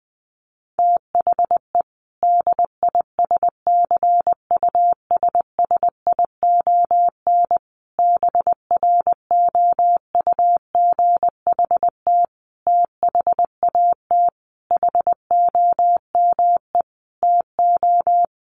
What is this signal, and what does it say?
Morse code, heard as a keyed tone, about 20 wpm, 700 Hz. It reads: THE DISCUSSION BROUGHT THAT HOME TO
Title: Sentences from Top 1100 Words 20wpm